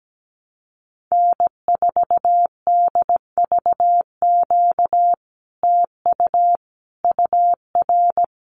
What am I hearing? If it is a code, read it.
Morse code, heard as a keyed tone, about 17 wpm, 700 Hz. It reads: N4DVQ TU UR
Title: Common QSO Phrases Encore 17wpm